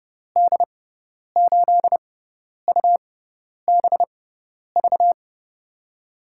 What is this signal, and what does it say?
Morse code, heard as a keyed tone, about 30 wpm, spaced as for 10 wpm, 700 Hz. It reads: D8UBV
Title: USA Call Signs - 6 Characters - 6x Character Spacing - 30wpm